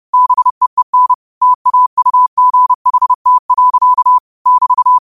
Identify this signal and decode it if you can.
Morse code, heard as a keyed tone, about 30 wpm, 1000 Hz. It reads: BEEN TAUGHT. <BT>
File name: qotdfiles / 30wpm